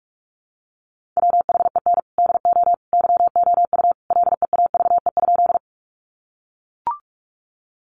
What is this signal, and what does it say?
Morse code, heard as a keyed tone, about 44 wpm, 700 Hz. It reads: WHEN DO YOU LEAVE?
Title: Sentences from Top 400 Words 45wpm